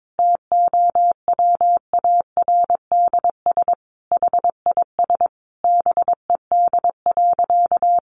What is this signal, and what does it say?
Morse code, heard as a keyed tone, about 22 wpm, 700 Hz. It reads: TOWARDS HIS BED.
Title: Sentences from Top 1100 Words 22wpm